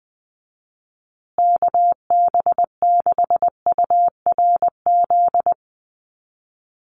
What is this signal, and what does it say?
Morse code, heard as a keyed tone, about 20 wpm, 700 Hz. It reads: KB6URZ
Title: USA Call Signs Repeated 3x 20wpm